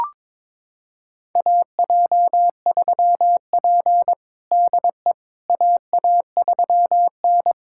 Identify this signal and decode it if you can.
Morse code, heard as a keyed tone, about 22 wpm, 700 Hz. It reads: AJ3P DE AA3N
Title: USA Callsign DE Callsign - 4 Characters - 22wpm